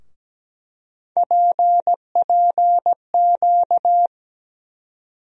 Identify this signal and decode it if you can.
Morse code, heard as a keyed tone, about 17 wpm, 700 Hz. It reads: PPQ
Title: USA Call Sign Postfixes - 3 Letters 17wpm